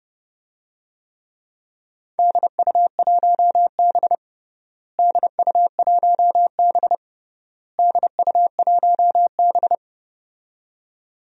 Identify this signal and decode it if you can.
Morse code, heard as a keyed tone, about 29 wpm, 700 Hz. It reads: DU1B DU1B DU1B
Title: DXCC-100 Call Signs - 4 Characters - Repeated 3x - 30wpm